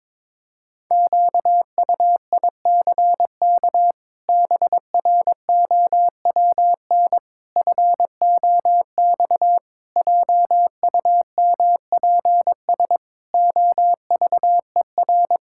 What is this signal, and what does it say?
Morse code, heard as a keyed tone, about 22 wpm, 700 Hz. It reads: QUICK BROWN FOX JUMPS OVER
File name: practice / 22wpm / Sets of 8 Words - 22wpm